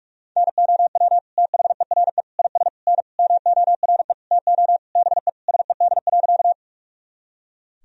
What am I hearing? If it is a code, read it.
Morse code, heard as a keyed tone, about 46 wpm, 700 Hz. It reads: NOW THERE IS N MORE TO BE SED.
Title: Sentences from Top 100 Words and Abbreviations 45wpm